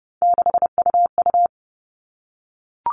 Morse code, heard as a keyed tone, about 30 wpm, 700 Hz. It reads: 6UU
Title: All Cut Numbers - 3-Digit 30wpm